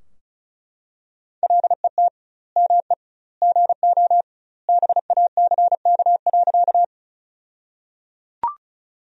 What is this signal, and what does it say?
Morse code, heard as a keyed tone, about 36 wpm, 700 Hz. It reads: LET ME GO BACK.
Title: 2x Word-Spacing - Sentences from Top 300 Words - 35wpm